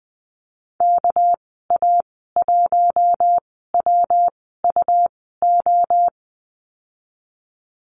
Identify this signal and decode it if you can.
Morse code, heard as a keyed tone, about 20 wpm, spaced as for 15 wpm, 700 Hz. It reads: KA1WUO
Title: 207 Lesson 33g - Letter j - Call Signs 20wpm